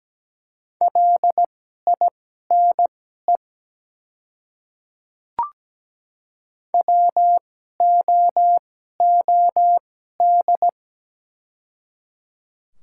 Morse code, heard as a keyed tone, about 17 wpm, spaced as for 12 wpm, 700 Hz. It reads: LINE WOOD
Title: 2x Character Spacing - Top 300 Words - 17wpm